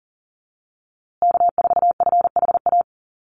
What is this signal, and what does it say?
Morse code, heard as a keyed tone, about 40 wpm, 700 Hz. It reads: K4FHA
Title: USA Call Signs Repeated 2x 40wpm